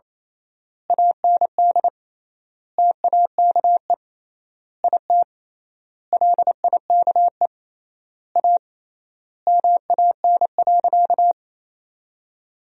Morse code, heard as a keyed tone, about 28 wpm, 700 Hz. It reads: AND TAKE IT LIKE A MAN.
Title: 3x Word-Spacing - Sentences from Top 300 Words - 28wpm